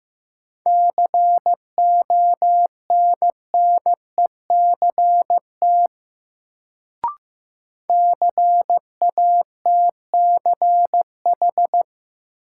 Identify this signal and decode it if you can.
Morse code, heard as a keyed tone, about 15 wpm, 700 Hz. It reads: CONNECT CATCH
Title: Sentences from Top 1500 Words - New Word Drill Alt - 15wpm